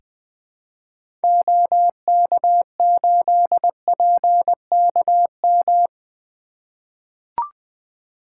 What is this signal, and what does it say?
Morse code, heard as a keyed tone, about 20 wpm, 700 Hz. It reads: OK8PKM
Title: 2023 SOTA Activator Call Signs - 20wpm